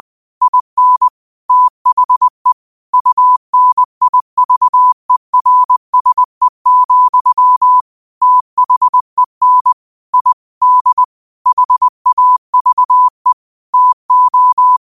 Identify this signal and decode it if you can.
Morse code, heard as a keyed tone, about 20 wpm, 1000 Hz. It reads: IN THE UNIVERSE, THEN I D HAVE TO